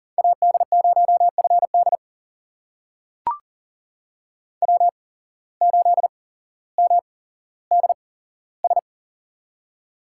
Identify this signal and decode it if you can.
Morse code, heard as a keyed tone, about 38 wpm, spaced as for 17 wpm, 700 Hz. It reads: AD0FD W8MDS
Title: USA Call Signs - 5 Characters - 8x Character Spacing - 40wpm